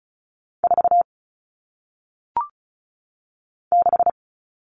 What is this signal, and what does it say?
Morse code, heard as a keyed tone, about 36 wpm, 700 Hz. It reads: 4 6
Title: All Cut Numbers - 1-Digit 35wpm